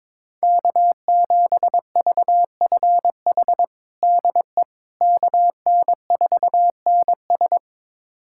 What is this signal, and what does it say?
Morse code, heard as a keyed tone, about 22 wpm, 700 Hz. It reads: K7VFH DE KN4NS
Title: USA Callsign DE Callsign - 5 Characters - 22wpm